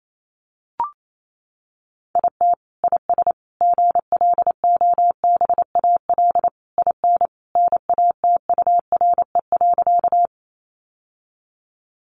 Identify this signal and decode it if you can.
Morse code, heard as a keyed tone, about 28 wpm, 700 Hz. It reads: IT IS GLOBAL IN NATURE.
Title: Sentences from Top 1200 Words 28wpm